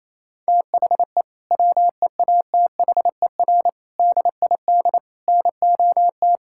Morse code, heard as a keyed tone, about 27 wpm, 700 Hz. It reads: THE WEATHER DID NOT
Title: Sentences from Top 1000 Words - Review Part 5 - 28wpm